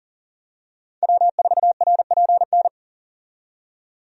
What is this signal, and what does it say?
Morse code, heard as a keyed tone, about 39 wpm, 700 Hz. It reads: W4RPN
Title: USA Call Signs - 5 Characters - 40wpm